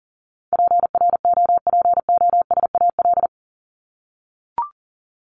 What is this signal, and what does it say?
Morse code, heard as a keyed tone, about 40 wpm, 700 Hz. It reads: PROPOSAL
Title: Top 1751-2000 Words 40wpm